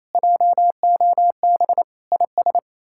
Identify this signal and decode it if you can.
Morse code, heard as a keyed tone, about 37 wpm, 700 Hz. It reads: JOB IS